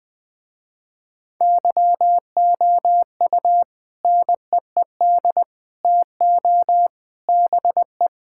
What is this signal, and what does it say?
Morse code, heard as a keyed tone, about 20 wpm, 700 Hz. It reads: YOU NEED TO BE
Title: Sentences with 3-Word Phrases - Spoken Beginning - 20wpm